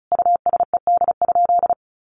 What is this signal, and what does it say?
Morse code, heard as a keyed tone, about 34 wpm, 700 Hz. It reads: USED?
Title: Sentences from Top 1200 Words 35wpm